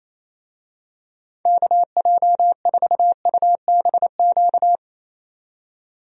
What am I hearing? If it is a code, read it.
Morse code, heard as a keyed tone, about 28 wpm, 700 Hz. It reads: KJ4UBQ
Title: USA Call Signs - 6 Characters - 28wpm